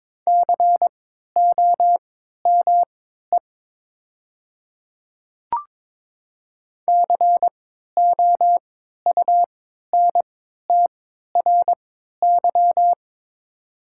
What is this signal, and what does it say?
Morse code, heard as a keyed tone, about 23 wpm, spaced as for 13 wpm, 700 Hz. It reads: COME COUNTRY
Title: 3x Character Spacing - Top 300 Words - 22wpm